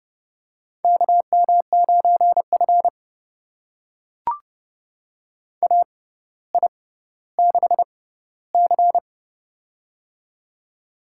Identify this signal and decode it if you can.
Morse code, heard as a keyed tone, about 30 wpm, spaced as for 15 wpm, 700 Hz. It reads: KM9F AI6C
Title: USA Call Signs - 4 Characters - 6x Character Spacing - 30wpm